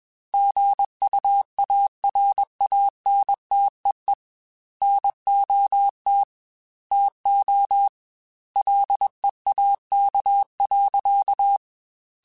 Morse code, heard as a keyed tone, about 21 wpm, 800 Hz. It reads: GUARANTEE NOT TO LEAK.